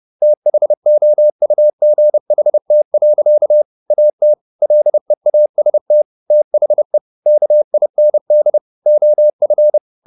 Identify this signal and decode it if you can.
Morse code, heard as a keyed tone, about 30 wpm, 600 Hz. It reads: THOUGHT. AT LEAST THE KIND OF